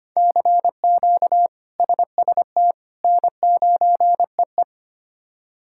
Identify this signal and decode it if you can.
Morse code, heard as a keyed tone, about 25 wpm, 700 Hz. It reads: CQ SST N9EE
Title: K1USN SST QSO Practice - 25wpm